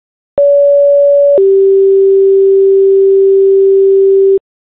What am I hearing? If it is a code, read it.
Two-tone page, heard as a one-second tone, then a three-second tone, about 570 then 390 Hz. It is heard from 0.4 to 4.4 s.